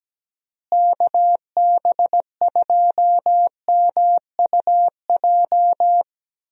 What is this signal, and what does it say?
Morse code, heard as a keyed tone, about 17 wpm, 700 Hz. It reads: KB2MUJ
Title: USA Call Signs - 6 Characters - Repeated 3x - 17wpm